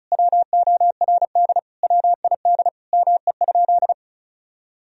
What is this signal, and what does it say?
Morse code, heard as a keyed tone, about 36 wpm, 700 Hz. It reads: WORD WID ME?
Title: Sentences from Top 300 Words and Abbreviations 35wpm